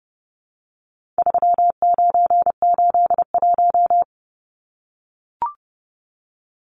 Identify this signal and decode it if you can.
Morse code, heard as a keyed tone, about 30 wpm, 700 Hz. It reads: 3981
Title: Numbers - 4-Digit 30wpm